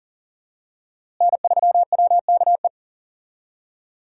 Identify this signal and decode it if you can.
Morse code, heard as a keyed tone, about 40 wpm, 700 Hz. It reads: N3WKE
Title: USA Call Signs - 5 Characters - Repeated 2x - 40wpm